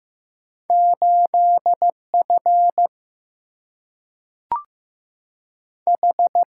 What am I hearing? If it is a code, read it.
Morse code, heard as a keyed tone, about 15 wpm, 700 Hz. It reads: 8 F H
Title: Single-Letter Single-Number 15wpm